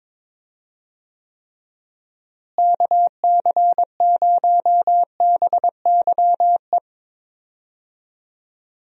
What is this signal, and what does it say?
Morse code, heard as a keyed tone, about 22 wpm, 700 Hz. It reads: KC0BYE
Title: USA Call Signs - 6 Characters - Repeated 2x - 22wpm